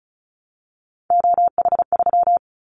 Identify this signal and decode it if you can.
Morse code, heard as a keyed tone, about 36 wpm, 700 Hz. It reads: OH3
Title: Callsign Prefixes for Top 50 DXCC - 35wpm